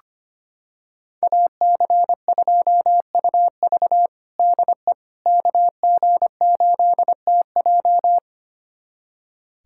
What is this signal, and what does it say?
Morse code, heard as a keyed tone, about 25 wpm, 700 Hz. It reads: AC2UV DE KG8TJ
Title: USA Callsign DE Callsign - 5 Characters - 25wpm